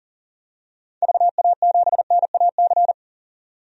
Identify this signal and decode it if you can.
Morse code, heard as a keyed tone, about 40 wpm, 700 Hz. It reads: VA7NAC
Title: DXCC-100 Call Signs - 6 Characters - 40wpm